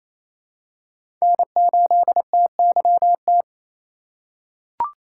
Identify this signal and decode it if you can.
Morse code, heard as a keyed tone, about 28 wpm, 700 Hz. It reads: N8TYT
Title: USA Call Signs - 5 Characters - 2x Character Spacing - 28wpm